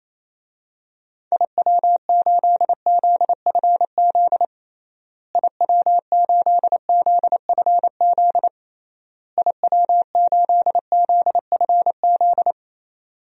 Morse code, heard as a keyed tone, about 28 wpm, 700 Hz. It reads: IW8ZFZ IW8ZFZ IW8ZFZ
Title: DXCC-100 Call Signs - 6 Characters - Repeated 3x - 28wpm